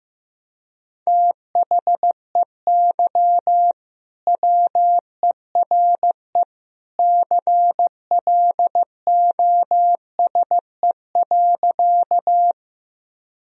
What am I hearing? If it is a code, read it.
Morse code, heard as a keyed tone, about 15 wpm, 700 Hz. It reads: THEY WERE CLOSE.
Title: Sentences from Top 1000 Words - Review Part 1 - 15wpm